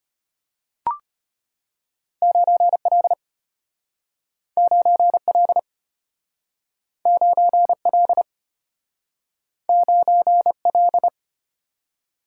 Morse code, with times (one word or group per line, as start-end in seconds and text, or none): 2.2-3.1 9L
4.6-5.6 9L
7.1-8.2 9L
9.7-11.1 9L